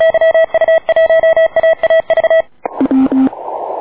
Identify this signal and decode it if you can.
Morse code, heard as a keyed tone, about 36 wpm, 1950 Hz, with noise. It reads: YU1AAV
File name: D4C@80CW